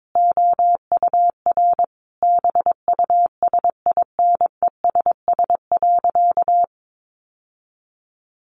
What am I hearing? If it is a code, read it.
Morse code, heard as a keyed tone, about 22 wpm, 700 Hz. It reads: OUR BUSINESS.
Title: Sentences from Top 1400 Words 22wpm